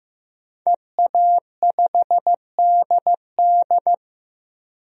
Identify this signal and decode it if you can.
Morse code, heard as a keyed tone, about 16 wpm, 700 Hz. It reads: EA5DD
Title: DXCC-100 Call Signs - 5 Characters - 15wpm